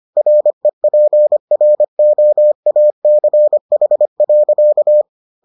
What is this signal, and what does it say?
Morse code, heard as a keyed tone, about 25 wpm, 600 Hz. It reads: REPROACH.